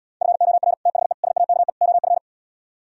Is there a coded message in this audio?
Yes: Morse code from 0.2 to 2.2 s, heard as a keyed tone, about 77 wpm, 700 Hz. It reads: FOR THE SAKE OF